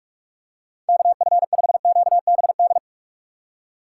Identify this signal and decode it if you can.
Morse code, heard as a keyed tone, about 42 wpm, 690 Hz. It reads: KF5QBD
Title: USA Call Signs - 6 Characters - 3x Character Spacing - 45wpm